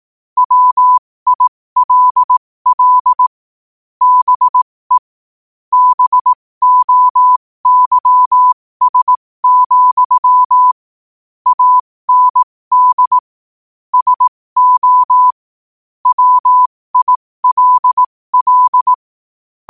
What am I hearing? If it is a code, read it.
Morse code, heard as a keyed tone, about 18 wpm, spaced as for 15 wpm, 1000 Hz. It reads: WILL BE BOYS, AND SO WILL
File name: qotdfiles / 15wpm